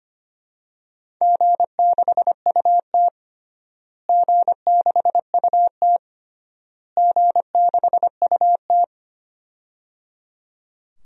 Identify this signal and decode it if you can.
Morse code, heard as a keyed tone, about 25 wpm, 700 Hz. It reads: G6UT G6UT G6UT
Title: DXCC-100 Call Signs - 4 Characters - Repeated 3x - 25wpm